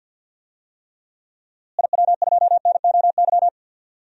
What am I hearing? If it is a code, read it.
Morse code, heard as a keyed tone, about 47 wpm, 700 Hz. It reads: IW2NOY
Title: DXCC-100 Call Signs - 6 Characters - Repeated 2x - 50wpm